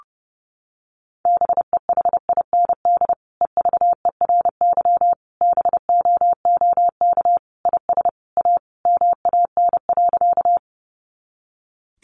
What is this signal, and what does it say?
Morse code, heard as a keyed tone, about 30 wpm, 700 Hz. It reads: BEHIND EVERY BOOK IS A MAN.
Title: Sentences from Top 500 Words 30wpm